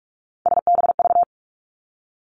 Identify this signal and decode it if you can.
Morse code, heard as a keyed tone, about 46 wpm, 700 Hz. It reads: SBV
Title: Three Challenging Letter-Number Combinations 45wpm